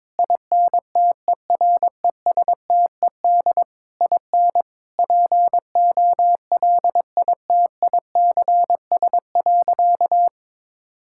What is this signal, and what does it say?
Morse code, heard as a keyed tone, about 21 wpm, 700 Hz. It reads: INTERESTED IN POLITICS.
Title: Sentences from Top 1000 Words - Review Part 6 - 22wpm